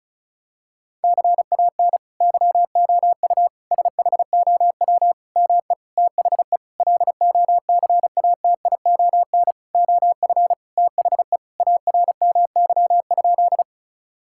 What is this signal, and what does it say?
Morse code, heard as a keyed tone, about 34 wpm, 700 Hz. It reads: CAN YOU SHOW ME THE LOCATION OF THE ARMY?
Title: Sentences from Top 1000 Words - Review Part 6 - 35wpm